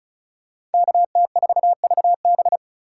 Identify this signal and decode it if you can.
Morse code, heard as a keyed tone, about 34 wpm, 700 Hz. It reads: KT4VB
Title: USA Call Signs - 5 Characters - 3x Character Spacing - 35wpm